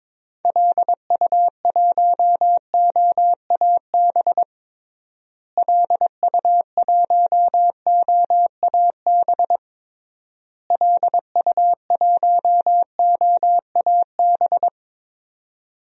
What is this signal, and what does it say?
Morse code, heard as a keyed tone, about 22 wpm, 700 Hz. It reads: LU1OAB LU1OAB LU1OAB
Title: DXCC-100 Call Signs - 6 Characters - Repeated 3x - 22wpm